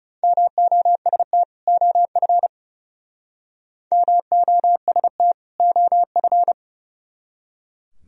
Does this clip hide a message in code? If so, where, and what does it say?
Yes: Morse code from 0.2 to 6.5 s, heard as a keyed tone, about 40 wpm, 700 Hz. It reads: MOST OF MOST OF